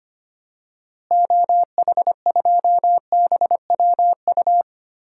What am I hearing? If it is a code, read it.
Morse code, heard as a keyed tone, about 25 wpm, 700 Hz. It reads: OH2BWU
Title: DXCC-100 Call Signs - 6 Characters - Repeated 2x - 25wpm